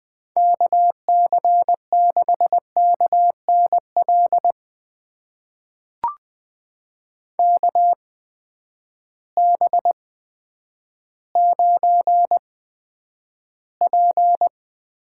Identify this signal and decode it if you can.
Morse code, heard as a keyed tone, about 20 wpm, spaced as for 10 wpm, 700 Hz. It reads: KC6KNL KB9P
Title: USA Call Signs - 8x Character Spacing - 20wpm